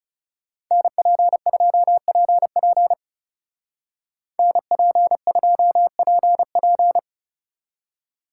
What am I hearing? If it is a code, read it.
Morse code, heard as a keyed tone, about 36 wpm, 700 Hz. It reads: NP2PP NP2PP
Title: Speed Racing - USA 2x2 Call Signs 30wpm